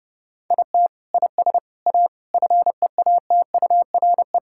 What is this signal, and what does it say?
Morse code, heard as a keyed tone, about 28 wpm, 700 Hz. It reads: IT IS A FEATURE
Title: Sentences from Top 1000 Words - Review Part 5 - 30wpm